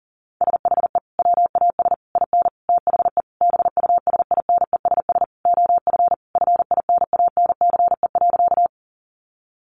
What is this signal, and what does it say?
Morse code, heard as a keyed tone, about 40 wpm, 700 Hz. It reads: SHE WAS IN THE BUSINESS OF FINANCE.
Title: Sentences from Top 1500 Words 40wpm